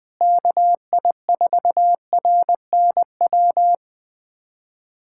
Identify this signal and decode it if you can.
Morse code, heard as a keyed tone, about 20 wpm, 700 Hz. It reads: KI4RNW
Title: USA Callsign DE Callsign - 6 Characters - 20wpm